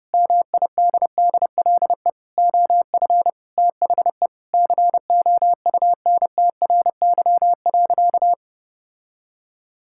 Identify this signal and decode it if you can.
Morse code, heard as a keyed tone, about 28 wpm, 700 Hz. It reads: MIDDLE OF THE COUNTRY.
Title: Sentences from Top 1000 Words - Review Part 6 - 30wpm